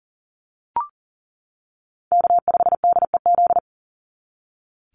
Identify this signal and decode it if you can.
Morse code, heard as a keyed tone, about 40 wpm, 700 Hz. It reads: K5DEZ
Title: NA SOTA Chaser Callsigns 40wpm